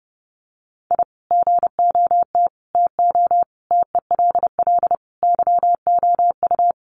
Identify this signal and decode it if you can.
Morse code, heard as a keyed tone, about 30 wpm, 700 Hz. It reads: I GOT TO TELL YOU
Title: Sets of 5 Words Encore2 30wpm